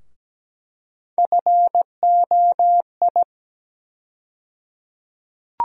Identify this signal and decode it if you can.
Morse code, heard as a keyed tone, about 17 wpm, 700 Hz. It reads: FOI
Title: USA Call Sign Postfixes - 3 Letters 17wpm